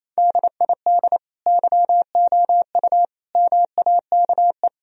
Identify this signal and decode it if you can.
Morse code, heard as a keyed tone, about 28 wpm, 700 Hz. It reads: DID YOU MAKE
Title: Sentences from Top 1000 Words - Review Part 4 - 28wpm